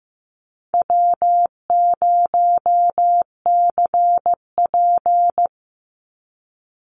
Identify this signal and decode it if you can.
Morse code, heard as a keyed tone, about 15 wpm, 700 Hz. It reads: W0CP
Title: NA SOTA Chaser Callsigns 15wpm